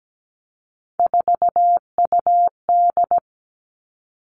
Morse code, heard as a keyed tone, about 17 wpm, 700 Hz. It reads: 4UD
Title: All Cut Numbers - 3-Digit 17wpm